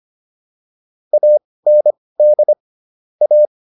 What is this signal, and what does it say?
Morse code, heard as a keyed tone, about 25 wpm, spaced as for 18 wpm, 600 Hz. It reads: AND A